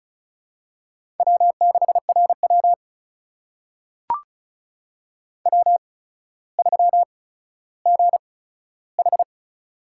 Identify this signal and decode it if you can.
Morse code, heard as a keyed tone, about 36 wpm, spaced as for 15 wpm, 700 Hz. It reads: W6RW W3GH
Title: USA Call Signs - 4 Characters - 8x Character Spacing - 35wpm